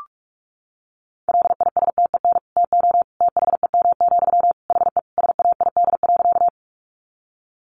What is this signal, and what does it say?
Morse code, heard as a keyed tone, about 44 wpm, 700 Hz. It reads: LISTEN TO THEM, HE SAID.
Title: Sentences from Top 400 Words 45wpm